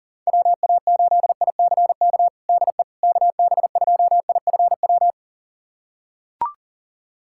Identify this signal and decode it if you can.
Morse code, heard as a keyed tone, about 40 wpm, 700 Hz. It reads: WA8ICK DE KB2IFW
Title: USA Callsign DE Callsign 40wpm